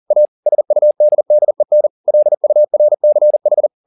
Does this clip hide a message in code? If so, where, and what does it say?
Yes: Morse code from 0.1 to 3.7 s, heard as a keyed tone, about 40 wpm, 600 Hz. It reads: A SUDDEN LURCH